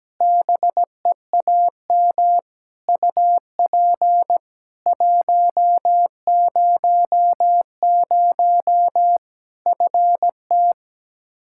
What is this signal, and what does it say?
Morse code, heard as a keyed tone, about 17 wpm, 700 Hz. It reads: BEAM UP 100 FT
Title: K1USN SST QSO Practice 17wpm